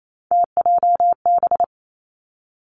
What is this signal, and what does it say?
Morse code, heard as a keyed tone, about 28 wpm, 690 Hz. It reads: TJB
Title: Three Challenging Letter-Number Combinations 28wpm